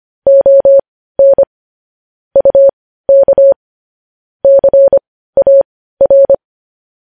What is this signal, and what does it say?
Morse code, heard as a keyed tone, about 25 wpm, spaced as for 15 wpm, 550 Hz. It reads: ON UK CAR
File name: podcast / News headlines-1x-25x15